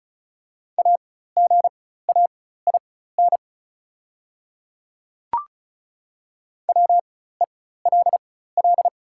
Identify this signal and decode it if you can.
Morse code, heard as a keyed tone, about 35 wpm, spaced as for 16 wpm, 690 Hz. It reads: AGAIN WELL
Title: 4x Character Spacing - Top 200 Words - 35wpm